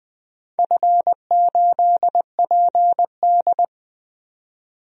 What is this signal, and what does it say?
Morse code, heard as a keyed tone, about 20 wpm, 700 Hz. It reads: F8PD
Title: DXCC-100 Call Signs - 4 Characters - 20wpm